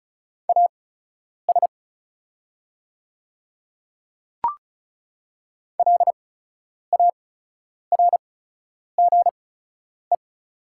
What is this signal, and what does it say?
Morse code, heard as a keyed tone, about 36 wpm, spaced as for 10 wpm, 700 Hz. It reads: AS LARGE
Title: 8x Character Spacing - Top 200 Words - 35wpm